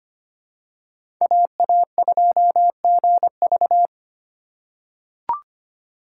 Morse code, heard as a keyed tone, about 25 wpm, 700 Hz. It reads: AA2GV
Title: USA Call Signs - 5 Characters - 4x Character Spacing - 25wpm